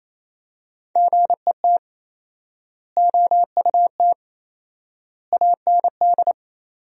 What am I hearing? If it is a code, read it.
Morse code, heard as a keyed tone, about 28 wpm, 700 Hz. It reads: GET OUT AND
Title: 4x Word-Spacing - Sentences from Top 200 Words - 28wpm